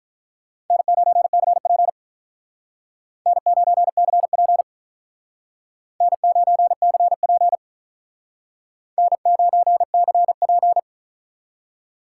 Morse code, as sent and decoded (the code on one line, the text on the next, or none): -. ----. -.-. .--. / -. ----. -.-. .--. / -. ----. -.-. .--. / -. ----. -.-. .--.
N9CP N9CP N9CP N9CP